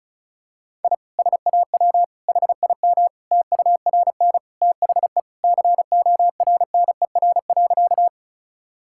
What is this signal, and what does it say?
Morse code, heard as a keyed tone, about 36 wpm, 700 Hz. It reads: I SAW HIM TURN THE CORNER.
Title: Sentences from Top 1600 Words 35wpm